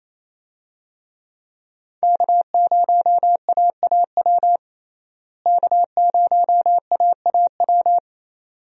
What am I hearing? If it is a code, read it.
Morse code, heard as a keyed tone, about 27 wpm, 700 Hz. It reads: K0AAW K0AAW
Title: USA Call Signs - 5 Characters - Repeated 2x - 28wpm